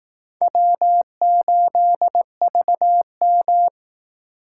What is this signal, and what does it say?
Morse code, heard as a keyed tone, about 18 wpm, 700 Hz. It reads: W8VM
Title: Speed Racing - USA 1x2 Call Signs 15wpm